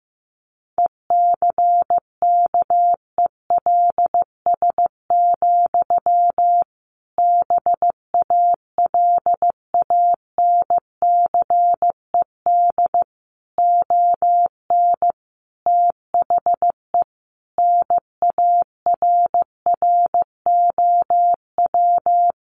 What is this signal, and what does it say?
Morse code, heard as a keyed tone, about 15 wpm, 700 Hz. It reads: ECKELS, BALANCED ON THE NARROW
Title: A Sound Of Thunder 15wpm